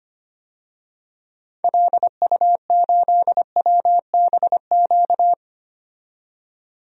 Morse code, heard as a keyed tone, about 25 wpm, 700 Hz. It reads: LU8WBQ
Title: DXCC-100 Call Signs - 6 Characters - 25wpm